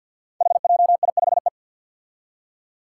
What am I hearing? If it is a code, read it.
Morse code, heard as a keyed tone, about 51 wpm, 700 Hz. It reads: HJI5E
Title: Five Challenging Letter-Number Combinations 50wpm